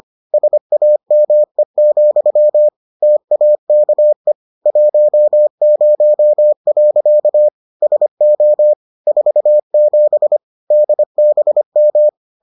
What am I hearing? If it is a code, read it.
Morse code, heard as a keyed tone, about 25 wpm, 600 Hz. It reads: SAME, TAKE 10. SO 47 DBM